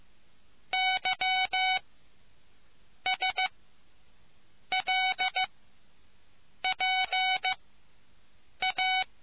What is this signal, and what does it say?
Morse code, heard as a keyed tone, about 15 wpm, spaced as for 6 wpm, 750 Hz. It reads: YSLPA